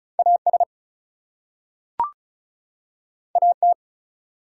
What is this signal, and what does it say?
Morse code, heard as a keyed tone, about 36 wpm, 700 Hz. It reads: AS AT
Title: Two Letter Words - 35wpm